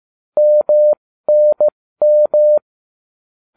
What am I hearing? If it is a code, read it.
Morse code, heard as a keyed tone, about 15 wpm, spaced as for 13 wpm, 600 Hz. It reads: MNM